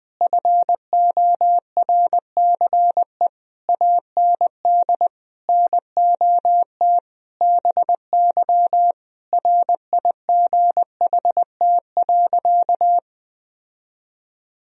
Morse code, heard as a keyed tone, about 20 wpm, 700 Hz. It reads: FORCE AND NOT BY RIGHT.
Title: Sentences from Top 1000 Words - Review Part 3 - 20wpm